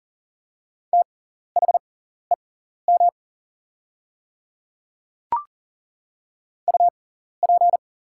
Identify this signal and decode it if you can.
Morse code, heard as a keyed tone, about 40 wpm, spaced as for 14 wpm, 700 Hz. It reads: THEM UP
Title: 6x Character Spacing - Top 100 Words - 40wpm